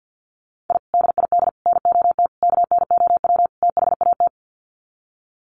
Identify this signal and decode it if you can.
Morse code, heard as a keyed tone, about 50 wpm, 700 Hz. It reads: I DID NOT KNOW THAT
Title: Sets of 5 Words 50wpm